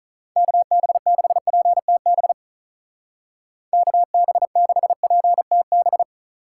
Morse code, heard as a keyed tone, about 44 wpm, 700 Hz. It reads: KB6PTB KB6PTB